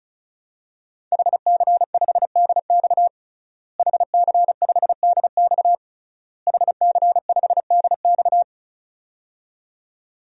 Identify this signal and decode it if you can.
Morse code, heard as a keyed tone, about 34 wpm, 700 Hz. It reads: HC5DX HC5DX HC5DX
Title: DXCC-100 Call Signs - 5 Characters - Repeated 3x - 35wpm